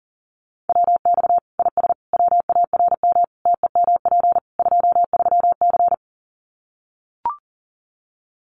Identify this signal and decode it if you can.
Morse code, heard as a keyed tone, about 40 wpm, 700 Hz. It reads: WX IS WARM TEMP 23C
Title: Common QSO Phrases 40wpm